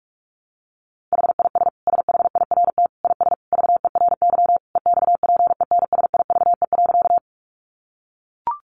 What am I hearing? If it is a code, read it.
Morse code, heard as a keyed tone, about 44 wpm, 700 Hz. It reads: HIS SHIRT IS VERY EXPENSIVE.
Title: Additional Easy Phrases 45wpm